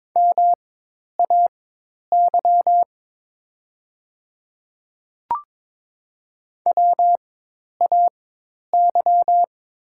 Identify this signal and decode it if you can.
Morse code, heard as a keyed tone, about 22 wpm, spaced as for 10 wpm, 690 Hz. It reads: MAY WAY
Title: 4x Character Spacing - Top 200 Words - 22wpm